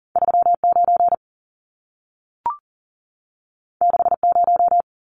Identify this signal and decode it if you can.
Morse code, heard as a keyed tone, about 40 wpm, 700 Hz. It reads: 39 60
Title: Numbers - 2-Digit 40wpm